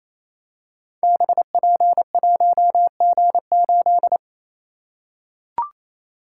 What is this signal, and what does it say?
Morse code, heard as a keyed tone, about 28 wpm, 700 Hz. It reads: BP1G8
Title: Code Groups - 5 Letter:Numbers - 28wpm